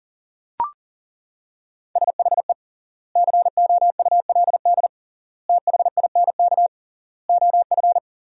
Morse code, heard as a keyed tone, about 39 wpm, 700 Hz. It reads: SHE COULD THINK OF
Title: 3x Word-Spacing - Sentences from Top 500 Words - 40wpm